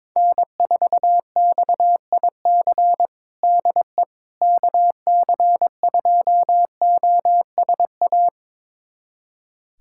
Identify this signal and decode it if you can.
Morse code, heard as a keyed tone, about 22 wpm, 700 Hz. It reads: N4XIC DE KC2OSA
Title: USA Callsign DE Callsign 22wpm